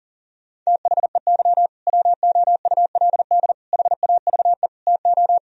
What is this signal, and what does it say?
Morse code, heard as a keyed tone, about 40 wpm, 700 Hz. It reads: THEY WOULD HAVE TO
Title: Sentences from Top 1000 Words - Review Part 1 - 40wpm